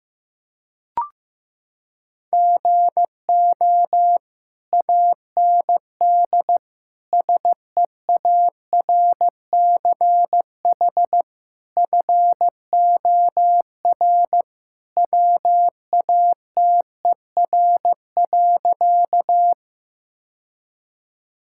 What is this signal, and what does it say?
Morse code, heard as a keyed tone, about 15 wpm, 700 Hz. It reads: GO AND SEARCH FOR WATER.
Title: Sentences from Top 1000 Words - Review Part 5 - 15wpm